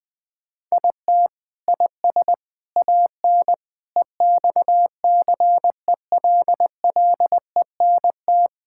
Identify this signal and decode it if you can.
Morse code, heard as a keyed tone, about 19 wpm, 700 Hz. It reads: IT IS AN EXCELLENT
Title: Sentences from Top 1000 Words - Review Part 5 - 20wpm